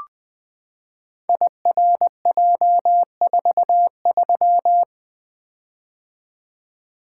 Morse code, heard as a keyed tone, about 20 wpm, 700 Hz. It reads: IRJ43
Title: Code Groups - 5 Letter:Numbers - 20wpm